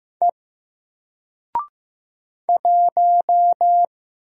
Morse code, heard as a keyed tone, about 15 wpm, 700 Hz. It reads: E 1
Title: Single Letter-Number 15wpm